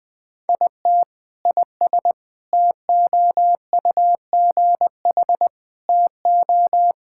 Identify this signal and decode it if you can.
Morse code, heard as a keyed tone, about 20 wpm, 700 Hz. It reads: IT IS TOUGH TO
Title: Sentences with 6-Word Phrases - Spoken Beginning - 20wpm